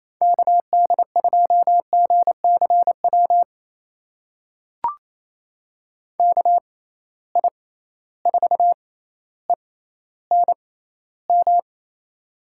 Morse code, heard as a keyed tone, about 28 wpm, spaced as for 14 wpm, 700 Hz. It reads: KD2GCW KI4ENM
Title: USA Call Signs - 6 Characters - 6x Character Spacing - 28wpm